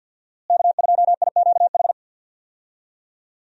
Morse code, heard as a keyed tone, about 46 wpm, 700 Hz. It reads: K2IQH
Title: USA Call Signs - 5 Characters - Repeated 3x - 50wpm